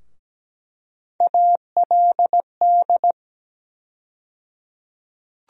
Morse code, heard as a keyed tone, about 17 wpm, 700 Hz. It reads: ALD
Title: USA Call Sign Postfixes - 3 Letters 17wpm